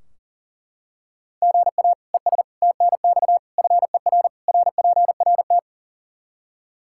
Morse code, heard as a keyed tone, about 40 wpm, 700 Hz. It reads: GA ES TNX FER RPRT
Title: Practice QSO - Exchange 1 - Beginning - 40wpm